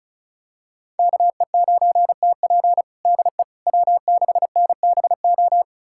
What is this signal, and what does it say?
Morse code, heard as a keyed tone, about 36 wpm, 700 Hz. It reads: KE9TP DE W6NBO
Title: USA Callsign DE Callsign - 5 Characters - 35wpm